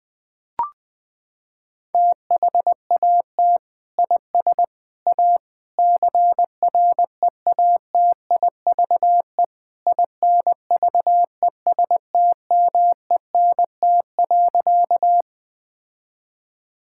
Morse code, heard as a keyed tone, about 20 wpm, 700 Hz. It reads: THAT IS A CREATIVE INVESTMENT.
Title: Sentences from Top 1000 Words - Review Part 5 - 20wpm